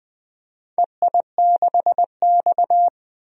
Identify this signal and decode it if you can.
Morse code, heard as a keyed tone, about 20 wpm, 700 Hz. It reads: EI6X
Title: DXCC-100 Call Signs - 4 Characters - Repeated 3x - 20wpm